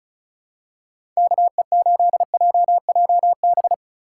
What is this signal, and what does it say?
Morse code, heard as a keyed tone, about 36 wpm, 700 Hz. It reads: KE8JJB
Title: USA Call Signs - 6 Characters - 4x Character Spacing - 35wpm